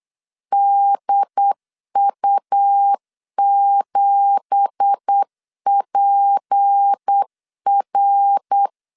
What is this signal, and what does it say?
Morse code, heard as a keyed tone, about 8 wpm, 800 Hz. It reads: DU7PR
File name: LEZIONI / 9 2867 8wpm 3440